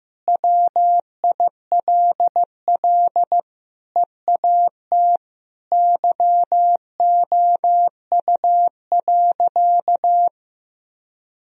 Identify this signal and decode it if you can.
Morse code, heard as a keyed tone, about 16 wpm, 700 Hz. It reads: WILL EAT YOU.
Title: Sentences from Top 1000 Words - Review Part 1 - 15wpm